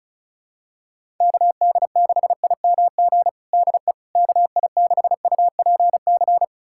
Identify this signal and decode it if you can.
Morse code, heard as a keyed tone, about 34 wpm, 690 Hz. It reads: KD6IMG DE KI6UPC
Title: USA Callsign DE Callsign 35wpm